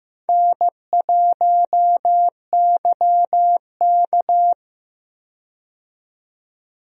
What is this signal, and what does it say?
Morse code, heard as a keyed tone, about 15 wpm, 700 Hz. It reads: N1YK
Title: USA Call Signs - 4 Characters - 15wpm